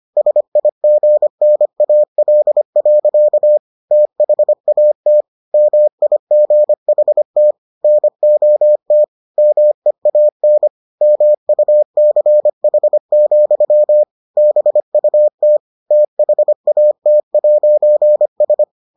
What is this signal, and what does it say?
Morse code, heard as a keyed tone, about 25 wpm, 600 Hz. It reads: SIGNAL. THAT MIGHT NOT MEAN MUCH, BUT THAT'S